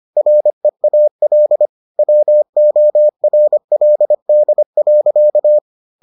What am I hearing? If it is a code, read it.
Morse code, heard as a keyed tone, about 25 wpm, 600 Hz. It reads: REAL WORLD.